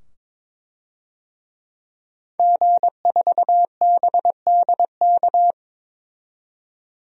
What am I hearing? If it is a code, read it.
Morse code, heard as a keyed tone, about 22 wpm, 700 Hz. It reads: G4BDK
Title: DXCC-100 Call Signs - 5 Characters - 22wpm